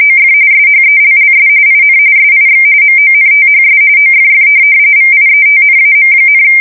RTTY, about 45.45 baud. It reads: OF SHELLS IN SIX MONTHS OFF VIETNAM,
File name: fsk170 / USS-New-Jersey